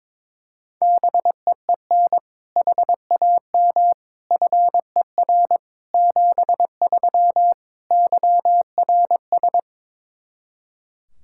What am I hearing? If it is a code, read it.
Morse code, heard as a keyed tone, about 22 wpm, 700 Hz. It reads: BEEN HAM FER 73 YRS
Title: Practice QSO - Exchange 3 - More Info - 22wpm